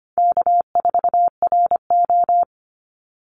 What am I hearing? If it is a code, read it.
Morse code, heard as a keyed tone, about 25 wpm, 700 Hz. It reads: K4RO
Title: Call Signs Repeated 3x - 25wpm